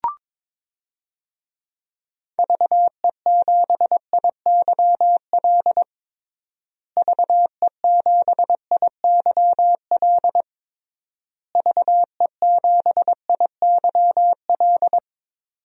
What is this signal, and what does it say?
Morse code, heard as a keyed tone, about 22 wpm, 700 Hz. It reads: VE7IYL VE7IYL VE7IYL
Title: DXCC-100 Call Signs - 6 Characters - Repeated 3x - 22wpm